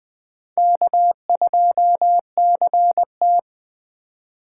Morse code, heard as a keyed tone, about 20 wpm, 700 Hz. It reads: K2CT
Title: USA Call Signs - 4 Characters - 8x Character Spacing - 20wpm